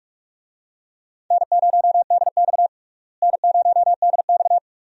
Morse code, heard as a keyed tone, about 42 wpm, 700 Hz. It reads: N0DX N0DX
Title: USA Call Signs - 4 Characters - Repeated 2x - 45wpm